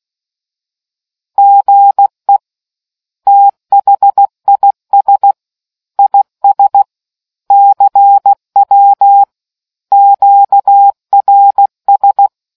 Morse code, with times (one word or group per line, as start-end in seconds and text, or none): 1.4-2.4 GE
3.3-5.3 THIS
6.0-6.8 IS
7.5-9.2 CW
9.9-12.3 QRS